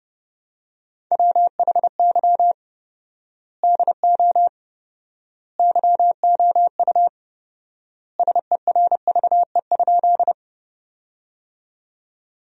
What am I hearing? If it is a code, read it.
Morse code, heard as a keyed tone, about 30 wpm, 700 Hz. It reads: WHY DO YOU SERVE?
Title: 4x Word-Spacing - Sentences from Top 400 Words - 30wpm